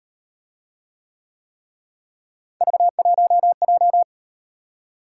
Morse code, heard as a keyed tone, about 38 wpm, 700 Hz. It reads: V1J